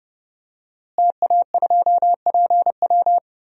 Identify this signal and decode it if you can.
Morse code, heard as a keyed tone, about 31 wpm, 700 Hz. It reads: TA2PW
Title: DXCC-100 Call Signs - 5 Characters - Repeated 2x - 30wpm